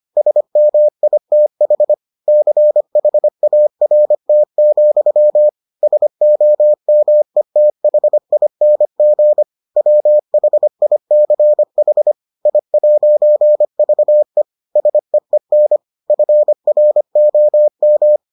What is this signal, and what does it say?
Morse code, heard as a keyed tone, about 25 wpm, 600 Hz. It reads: SMITH CHART, SOMETHING WHICH I'VE SEEN FROM